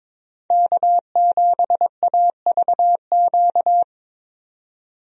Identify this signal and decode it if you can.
Morse code, heard as a keyed tone, about 22 wpm, 700 Hz. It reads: K7AVQ
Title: USA Call Signs - 5 Characters - 22wpm